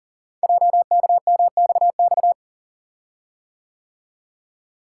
Morse code, heard as a keyed tone, about 42 wpm, 700 Hz. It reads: JKMXX
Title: Code Groups - 5 Letters - 40wpm